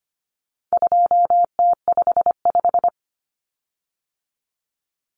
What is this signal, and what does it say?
Morse code, heard as a keyed tone, about 25 wpm, 700 Hz. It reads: 2T55
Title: Common Cut Numbers - 4-Digit 25wpm